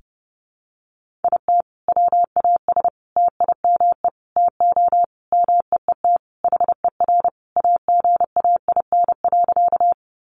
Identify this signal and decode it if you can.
Morse code, heard as a keyed tone, about 30 wpm, 700 Hz. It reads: IT WAS TIME TO MEET HER AGAIN.
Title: Sentences from Top 1100 Words 30wpm